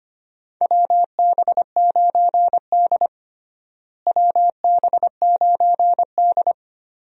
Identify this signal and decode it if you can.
Morse code, heard as a keyed tone, about 25 wpm, 700 Hz. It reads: WB9D WB9D
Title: USA Call Signs - 4 Characters - Repeated 2x - 25wpm